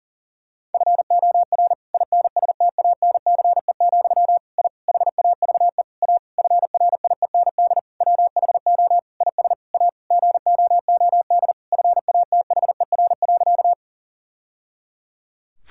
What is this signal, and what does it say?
Morse code, heard as a keyed tone, about 37 wpm, 700 Hz. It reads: FOR INSTANCE, I HAVE A FRIEND WHO IS A GOOD FATHER.
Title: Sentences from Top 1000 Words - Review Part 5 - 40wpm